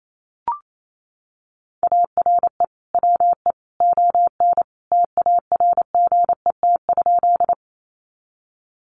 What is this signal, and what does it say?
Morse code, heard as a keyed tone, about 28 wpm, 700 Hz. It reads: ARE WE ON TARGET?
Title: Sentences from Top 1200 Words 28wpm